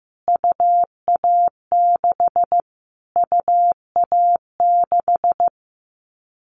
Morse code, heard as a keyed tone, about 15 wpm, 700 Hz. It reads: UA6 UA6
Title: Callsign Prefixes for Top 25 DXCC - Repeated 3x - 15wpm